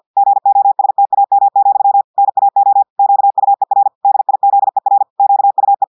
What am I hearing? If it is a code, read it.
Morse code, heard as a keyed tone, about 50 wpm, 800 Hz. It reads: GOSTAM, NAO QUER DIZER QUE
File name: audios / portugais / portugais texte en clair 2 vitesse 50 mots minutes